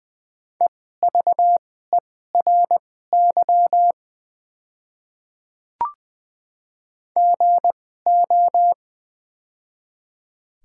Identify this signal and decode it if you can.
Morse code, heard as a keyed tone, about 20 wpm, spaced as for 14 wpm, 700 Hz. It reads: EVERY GO
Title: 2x Character Spacing - Top 300 Words - 20wpm